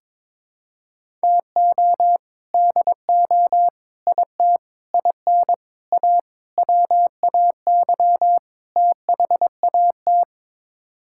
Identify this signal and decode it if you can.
Morse code, heard as a keyed tone, about 22 wpm, 700 Hz. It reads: TO DO IT IN A WAY THAT
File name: practice / 22wpm / Sets of 7 Words - 22wpm